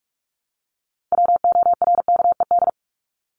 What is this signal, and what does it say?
Morse code, heard as a keyed tone, about 44 wpm, 700 Hz. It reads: WORKED
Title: ED Suffix - Top 100 Words with Base Words 45wpm